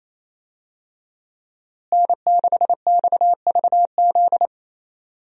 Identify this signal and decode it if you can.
Morse code, heard as a keyed tone, about 28 wpm, 700 Hz. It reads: N6XVZ
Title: USA Call Signs - 5 Characters - Repeated 3x - 28wpm